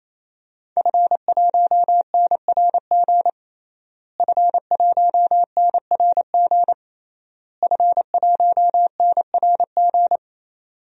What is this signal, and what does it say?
Morse code, heard as a keyed tone, about 28 wpm, 700 Hz. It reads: F1NRG F1NRG F1NRG
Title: DXCC-100 Call Signs - 5 Characters - Repeated 3x - 28wpm